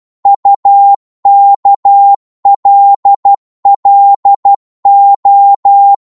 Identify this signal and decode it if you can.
Morse code, heard as a keyed tone, about 12 wpm, 800 Hz. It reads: UKLLO
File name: Morse / ltr-12wpm